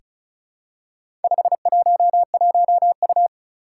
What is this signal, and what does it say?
Morse code, heard as a keyed tone, about 36 wpm, 700 Hz. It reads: 511U
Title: Four Challenging Letter-Number Combinations 35wpm